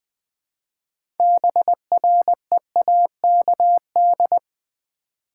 Morse code, heard as a keyed tone, about 20 wpm, 700 Hz. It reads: BREAKD
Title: Build A Word - 3 Letters At A Time - 9-Letter Words - 20wpm